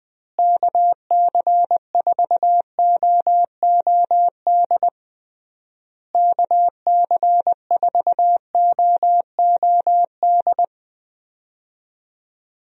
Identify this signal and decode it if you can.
Morse code, heard as a keyed tone, about 20 wpm, 700 Hz. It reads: KC4OOD KC4OOD
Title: USA Call Signs - 6 Characters - Repeated 3x - 20wpm